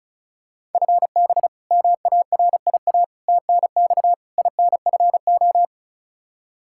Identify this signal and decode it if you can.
Morse code, heard as a keyed tone, about 35 wpm, 700 Hz. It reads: FB MARIA TNX INFO
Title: K1USN SST QSO Practice 35wpm